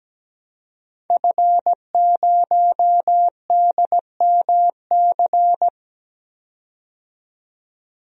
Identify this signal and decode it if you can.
Morse code, heard as a keyed tone, about 17 wpm, 700 Hz. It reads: F0DMC
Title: DXCC-100 Call Signs - 5 Characters - 17wpm